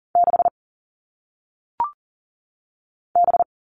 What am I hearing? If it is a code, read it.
Morse code, heard as a keyed tone, about 40 wpm, 700 Hz. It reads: 6 B
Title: All Cut Numbers - 1-Digit 40wpm